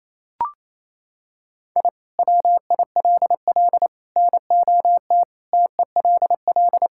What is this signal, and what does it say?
Morse code, heard as a keyed tone, about 28 wpm, 700 Hz. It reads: I WILL NOT TELL
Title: Sentences from Top 1000 Words - Review Part 5 - 28wpm